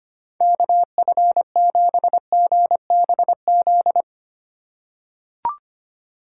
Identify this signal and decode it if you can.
Morse code, heard as a keyed tone, about 25 wpm, 700 Hz. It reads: KF7GBZ
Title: USA Call Signs - 6 Characters - 3x Character Spacing - 25wpm